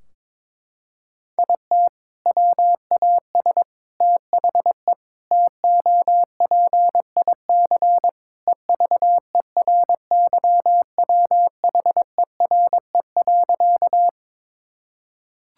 Morse code, heard as a keyed tone, about 22 wpm, 700 Hz. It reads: IT WAS THE TOPIC EVERYWHERE.
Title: Sentences from Top 1600 Words 22wpm